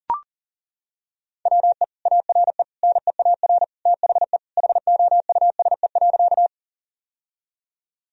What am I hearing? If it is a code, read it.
Morse code, heard as a keyed tone, about 40 wpm, 700 Hz. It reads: WE ARE NEAR THE HOUSE.
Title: Sentences from Top 1000 Words - Review Part 1 - 40wpm